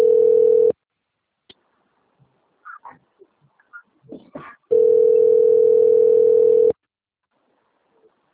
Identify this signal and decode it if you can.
Ringback tone (North America), 440 + 480 Hz.